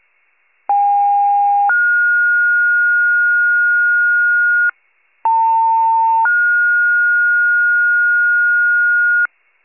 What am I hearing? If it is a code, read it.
Two-tone page, heard as a one-second tone, then a three-second tone, about 810 then 1430 Hz, about 900 then 1430 Hz. It is heard from 0.7 to 9.3 s.
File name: wcalert